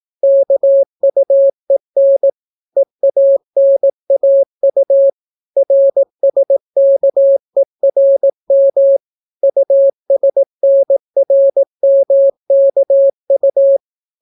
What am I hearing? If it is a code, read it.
Morse code, heard as a keyed tone, about 18 wpm, 550 Hz. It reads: KUEN EANAU RSKERM USNRMKU